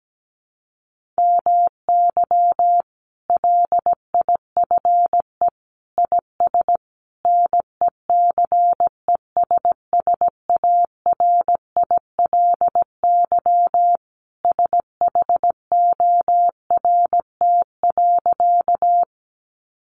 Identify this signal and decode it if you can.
Morse code, heard as a keyed tone, about 17 wpm, 700 Hz. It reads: MY LIFE IS NECESSARILY SHORT.
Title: Sentences from Top 1400 Words 17wpm